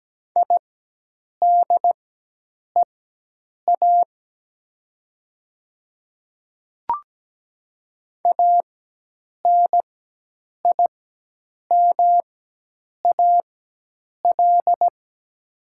Morse code, heard as a keyed tone, about 17 wpm, spaced as for 8 wpm, 700 Hz. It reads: IDEA ANIMAL
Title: 4x Character Spacing - Top 300 Words - 17wpm